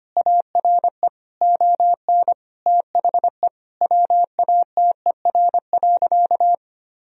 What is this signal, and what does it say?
Morse code, heard as a keyed tone, about 25 wpm, 700 Hz. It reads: ARE ON THE WATER.
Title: Sentences from Top 1000 Words - Review Part 1 - 25wpm